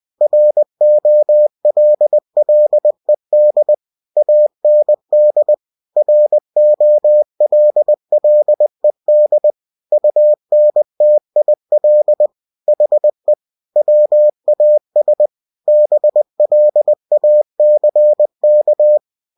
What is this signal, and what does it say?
Morse code, heard as a keyed tone, about 20 wpm, 600 Hz. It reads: ROLLED AND ROLLED UNTIL HE WAS BLACK
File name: cw-books / winnie-the-pooh / winnie-20wpm-page-0006